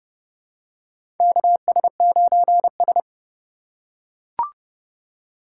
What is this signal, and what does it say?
Morse code, heard as a keyed tone, about 30 wpm, 700 Hz. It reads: KS9S
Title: USA Call Signs - 4 Characters - 3x Character Spacing - 30wpm